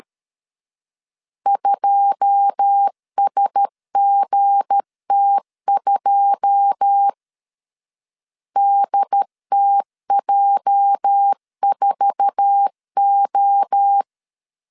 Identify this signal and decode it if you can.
Morse code, heard as a keyed tone, about 13 wpm, 800 Hz. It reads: 2SGT2 DTJ4O
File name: LEZIONI / 9 2 12wpm 2301